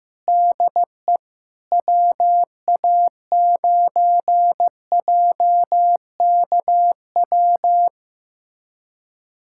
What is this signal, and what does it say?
Morse code, heard as a keyed tone, about 16 wpm, 700 Hz. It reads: DE WA9JKW
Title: USA Callsign DE Callsign - 6 Characters - 15wpm